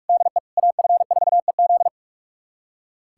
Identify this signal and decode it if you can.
Morse code, heard as a keyed tone, about 45 wpm, 700 Hz. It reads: DE AF4EZ
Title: USA Callsign DE Callsign - 5 Characters - 45wpm